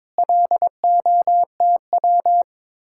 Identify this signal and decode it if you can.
Morse code, heard as a keyed tone, about 22 wpm, 700 Hz. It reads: LOTW
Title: Speed-Racing Common QSO Phrases - 22wpm